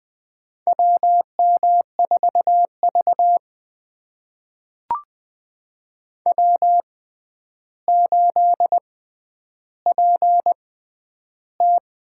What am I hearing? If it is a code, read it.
Morse code, heard as a keyed tone, about 20 wpm, spaced as for 10 wpm, 700 Hz. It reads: WM4V W8PT
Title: USA Call Signs - 4 Characters - 6x Character Spacing - 20wpm